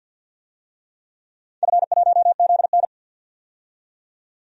Morse code, heard as a keyed tone, about 52 wpm, 700 Hz. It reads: F1ZN
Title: DXCC-100 Call Signs - 4 Characters - 50wpm